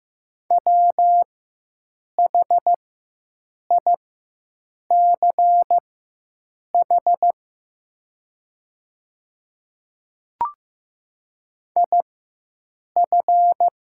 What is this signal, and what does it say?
Morse code, heard as a keyed tone, about 15 wpm, spaced as for 7 wpm, 700 Hz. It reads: WHICH IF
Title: 4x Character Spacing - Top 100 Words - 15wpm